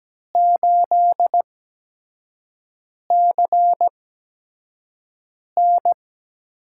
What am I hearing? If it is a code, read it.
Morse code, heard as a keyed tone, about 17 wpm, spaced as for 5 wpm, 700 Hz. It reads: 8CN
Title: USA Call Signs - 8x Character Spacing - 17wpm